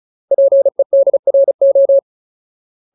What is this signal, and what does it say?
Morse code, heard as a keyed tone, about 34 wpm, 550 Hz. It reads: PEDRO
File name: Ultimate Word List / 35WPM Ultimate